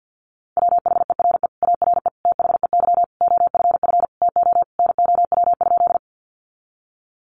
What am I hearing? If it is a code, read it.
Morse code, heard as a keyed tone, about 50 wpm, 700 Hz. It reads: WHERE ARE THEY OFF TO NOW?
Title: Sentences from Top 200 Words 50wpm